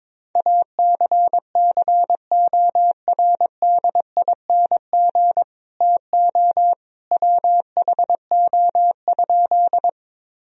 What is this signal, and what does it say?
Morse code, heard as a keyed tone, about 22 wpm, 700 Hz. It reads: ACCORDING TO WHO?
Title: Sentences from Top 1000 Words - Review Part 4 - 22wpm